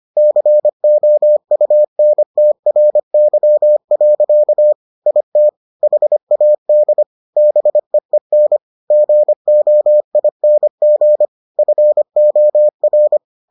Morse code, heard as a keyed tone, about 25 wpm, 600 Hz. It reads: COUNTRY. IT HAD BEEN GOING FOR